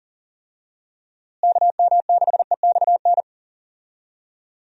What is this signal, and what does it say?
Morse code, heard as a keyed tone, about 40 wpm, 700 Hz. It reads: KM6EXN
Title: USA Call Signs - 6 Characters - Repeated 2x - 40wpm